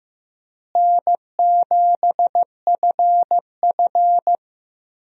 Morse code, heard as a keyed tone, about 15 wpm, 700 Hz. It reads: N7FF
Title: USA Call Signs - 4 Characters - Repeated 2x - 15wpm